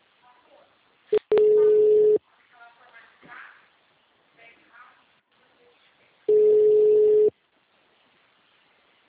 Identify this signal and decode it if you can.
Ringback tone (Europe, 425 Hz), with speech.